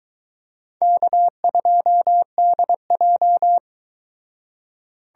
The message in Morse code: K2DJ

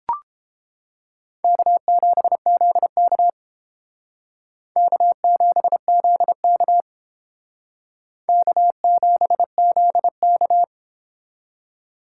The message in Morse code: K7ZK K7ZK K7ZK

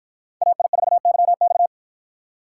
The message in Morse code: AI4YX

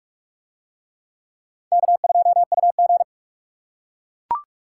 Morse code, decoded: K2UG

